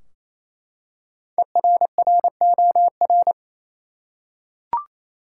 Morse code, heard as a keyed tone, about 28 wpm, 700 Hz. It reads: ERROR
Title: Five Letter Words - 28wpm